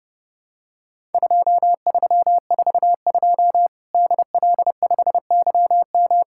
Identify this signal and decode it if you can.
Morse code, heard as a keyed tone, about 30 wpm, 700 Hz. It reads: 2342 DL5YM
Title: Worked All Europe DX Contest - QTC Exchange - 30wpm